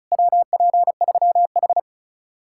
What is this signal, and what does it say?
Morse code, heard as a keyed tone, about 45 wpm, 700 Hz. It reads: WP3H